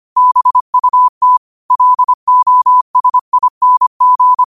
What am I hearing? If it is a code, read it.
Morse code, heard as a keyed tone, about 25 wpm, 1000 Hz. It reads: BUT LOSING